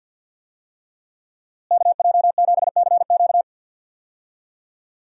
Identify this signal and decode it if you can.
Morse code, heard as a keyed tone, about 52 wpm, 700 Hz. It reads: KJ7CQ
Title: USA Call Signs - 5 Characters - Repeated 3x - 50wpm